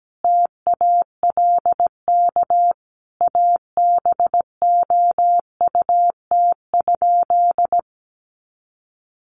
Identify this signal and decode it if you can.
Morse code, heard as a keyed tone, about 17 wpm, 700 Hz. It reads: TALK ABOUT?
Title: Sentences from Top 400 Words 17wpm